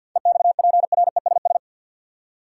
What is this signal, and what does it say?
Morse code, heard as a keyed tone, about 53 wpm, 700 Hz. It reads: EXPRESS
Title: Seven Letter Words - 50wpm